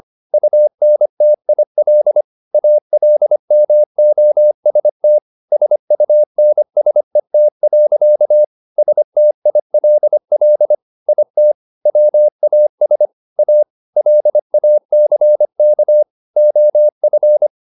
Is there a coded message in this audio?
Yes: Morse code from 0.3 to 17.5 s, heard as a keyed tone, about 25 wpm, 600 Hz. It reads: UNTIL ALMOST SUNSET. STILL IT WAS A LACK OF